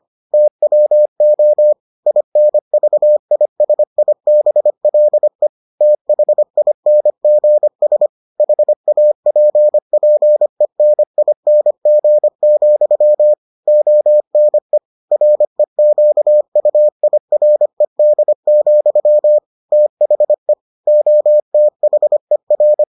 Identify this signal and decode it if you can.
Morse code, heard as a keyed tone, about 25 wpm, 600 Hz. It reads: TWO INVISIBLE THINGS HAPPENING, ONE REQUIRED, THE OTHER